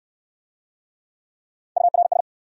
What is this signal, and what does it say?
Morse code, heard as a keyed tone, about 66 wpm, 700 Hz. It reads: URS